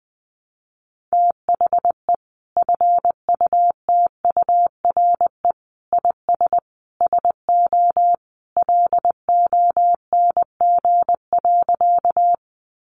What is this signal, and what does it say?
Morse code, heard as a keyed tone, about 20 wpm, 700 Hz. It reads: THE FUTURE IS SO LONG.
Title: Sentences from Top 700 Words 20wpm